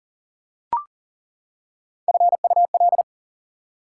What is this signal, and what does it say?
Morse code, heard as a keyed tone, about 40 wpm, 700 Hz. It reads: FUL
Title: Prefixes and Suffixes 40wpm